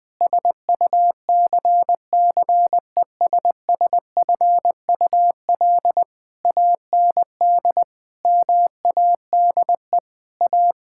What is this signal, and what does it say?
Morse code, heard as a keyed tone, about 21 wpm, 700 Hz. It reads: SUCCESSFUL AND MADE A
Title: Sentences from Top 1000 Words - Review Part 6 - 20wpm